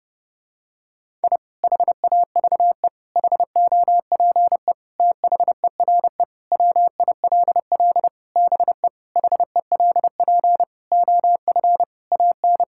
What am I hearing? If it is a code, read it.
Morse code, heard as a keyed tone, about 30 wpm, 700 Hz. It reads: I HAVE HOPE THERE WILL BE HELP OF AN
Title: Sentences from Top 1000 Words - Review Part 5 - 30wpm